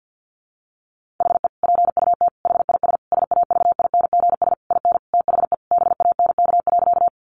Morse code, heard as a keyed tone, about 50 wpm, 700 Hz. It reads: HE PUT HIS SAVINGS IN THE BANK.
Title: Sentences from Top 1200 Words 50wpm